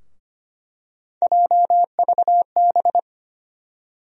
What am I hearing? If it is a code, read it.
Morse code, heard as a keyed tone, about 25 wpm, 700 Hz. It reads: JVB
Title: USA Call Sign Postfixes - 3 Letters 25wpm